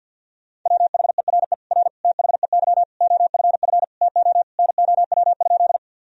Morse code, heard as a keyed tone, about 50 wpm, 700 Hz. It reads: WHERE R THEY OFF TO NOW?
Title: Sentences from Top 200 Words and Abbreviations 50wpm